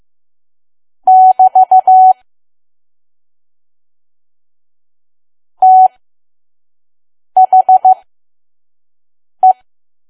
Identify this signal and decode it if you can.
Morse code, heard as a keyed tone, about 15 wpm, spaced as for 5 wpm, 750 Hz. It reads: <BT> THE
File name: Archive / 5 WPM files / PM